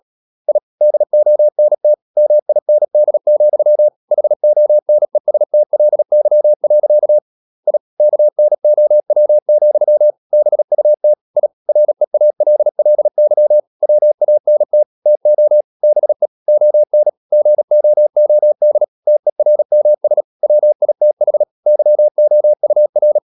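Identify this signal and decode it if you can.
Morse code, heard as a keyed tone, about 37 wpm, 600 Hz. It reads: I DONT MIND, HONESTLY. I KNOW, BUT I REALLY WANT TO BE ON GOOD TERMS WITH YOUR
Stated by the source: Morse practice at 36 wpm, text of Ken Follett's novel Never, from CW med Gnister.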